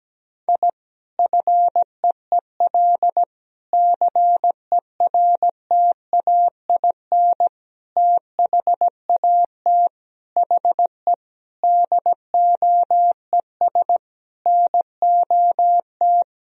Morse code, heard as a keyed tone, about 17 wpm, 700 Hz. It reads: I FEEL CERTAIN THAT HE DOES NOT
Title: Sentences from Top 1000 Words - Review Part 2 - 17wpm